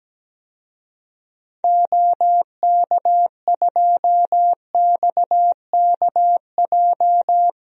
Morse code, heard as a keyed tone, about 17 wpm, 700 Hz. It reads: OK2XKJ
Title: DXCC-100 Call Signs - 6 Characters - 17wpm